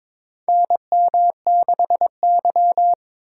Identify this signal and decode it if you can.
Morse code, heard as a keyed tone, about 22 wpm, 700 Hz. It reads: NM6Y
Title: USA Call Signs - 4 Characters - 8x Character Spacing - 22wpm